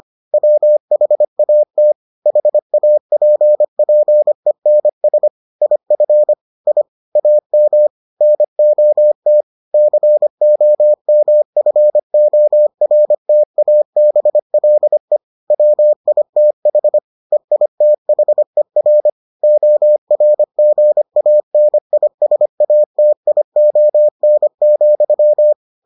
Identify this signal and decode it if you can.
Morse code, heard as a keyed tone, about 25 wpm, 600 Hz. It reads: WHAT HAPPENS IF I AM NOT COMFORTABLE WITH EITHER ORGANISATION,